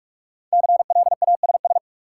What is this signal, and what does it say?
Morse code, heard as a keyed tone, about 45 wpm, 700 Hz. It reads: CLASS
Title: Sentences from Top 1000 Words - Review Part 2 - New Word Drill - 45wpm